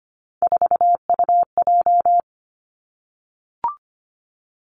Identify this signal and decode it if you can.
Morse code, heard as a keyed tone, about 25 wpm, 700 Hz. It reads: 4UJ
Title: Three Challenging Letter-Number Combinations 25wpm